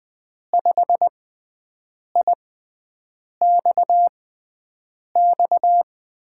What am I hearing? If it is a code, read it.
Morse code, heard as a keyed tone, about 20 wpm, spaced as for 7 wpm, 700 Hz. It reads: 5IXX
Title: USA Call Signs - 6 Characters - 6x Character Spacing - 20wpm